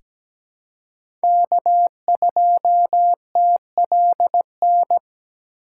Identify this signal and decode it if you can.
Morse code, heard as a keyed tone, about 17 wpm, 700 Hz. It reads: K2TLN
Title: USA Call Signs - 5 Characters - 4x Character Spacing - 17wpm